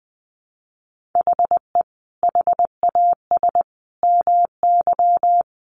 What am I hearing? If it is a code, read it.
Morse code, heard as a keyed tone, about 19 wpm, 700 Hz. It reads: HE HAS MY
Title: Sentences from Top 1500 Words 20wpm